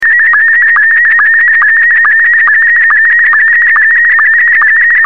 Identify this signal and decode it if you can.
SSTV (slow-scan television), Scottie 1.